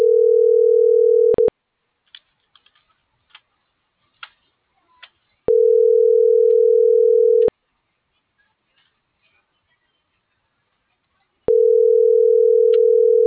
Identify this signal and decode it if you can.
Ringback tone (North America), 440 + 480 Hz.